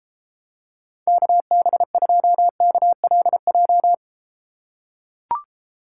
Morse code, heard as a keyed tone, about 33 wpm, 700 Hz. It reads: KB2KLJ